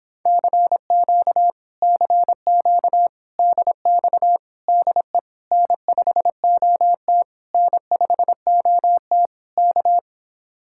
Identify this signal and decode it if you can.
Morse code, heard as a keyed tone, about 26 wpm, 700 Hz. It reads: CQ CQ DX DE N5OT N5OT K